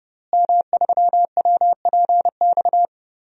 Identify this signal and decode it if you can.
Morse code, heard as a keyed tone, about 32 wpm, 700 Hz. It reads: M3WPX
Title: DXCC-100 Call Signs - 5 Characters - Repeated 3x - 30wpm